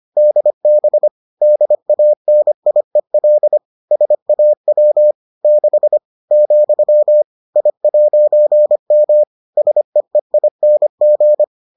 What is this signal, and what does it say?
Morse code, heard as a keyed tone, about 25 wpm, 600 Hz. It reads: DB DANIEL SAW 6 , I'M SEEING